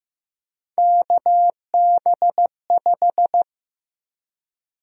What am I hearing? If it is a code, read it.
Morse code, heard as a keyed tone, about 15 wpm, 700 Hz. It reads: KB5
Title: USA Call Sign Prefixes - Repeated 2x - 15wpm